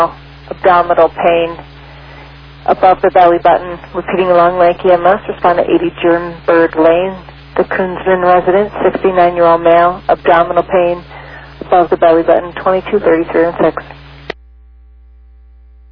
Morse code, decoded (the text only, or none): none